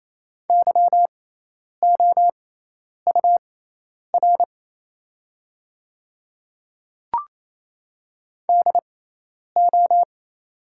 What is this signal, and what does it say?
Morse code, heard as a keyed tone, about 28 wpm, spaced as for 10 wpm, 700 Hz. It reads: YOUR DO
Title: 6x Character Spacing - Top 300 Words - 28wpm